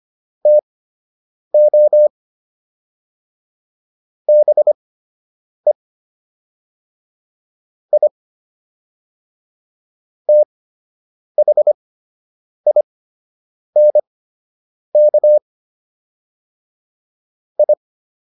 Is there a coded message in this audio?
Yes: Morse code from 0.5 to 17.7 s, heard as a keyed tone, about 25 wpm, spaced as for 8 wpm, 600 Hz. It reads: TO BE I THINK I